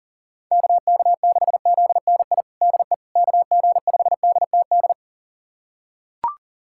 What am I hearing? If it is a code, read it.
Morse code, heard as a keyed tone, about 37 wpm, 700 Hz. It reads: KK6ZNI DE KG5DTD
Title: USA Callsign DE Callsign - 6 Characters - 40wpm